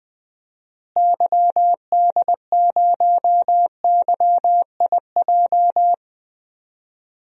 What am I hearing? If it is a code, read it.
Morse code, heard as a keyed tone, about 21 wpm, 700 Hz. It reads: YD0YIJ
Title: DXCC-100 Call Signs - 6 Characters - Repeated 3x - 20wpm